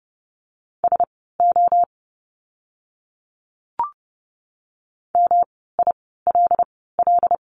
Morse code, heard as a keyed tone, about 30 wpm, spaced as for 17 wpm, 700 Hz. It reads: SO MILL
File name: course / 30:10wpm / 108 Lesson 18f - Letter m - Words 30wpm